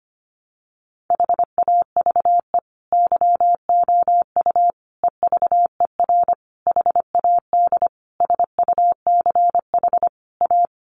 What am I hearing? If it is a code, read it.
Morse code, heard as a keyed tone, about 25 wpm, 700 Hz. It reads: HAVE YOU EVER HAD SUCH A
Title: Sentences from Top 1200 Words 25wpm